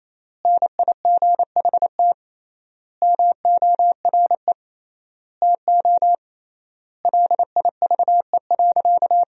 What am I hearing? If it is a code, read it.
Morse code, heard as a keyed tone, about 28 wpm, 700 Hz. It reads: NIGHT MORE TO LIVE.
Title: 3x Word-Spacing - Sentences from Top 300 Words - 28wpm